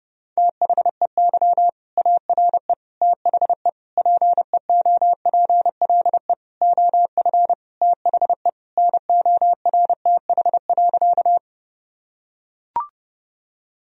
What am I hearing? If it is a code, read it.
Morse code, heard as a keyed tone, about 30 wpm, 700 Hz. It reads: THEY ARE THE PEOPLE OF THE NORTH.
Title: Sentences from Top 1000 Words - Review Part 1 - 30wpm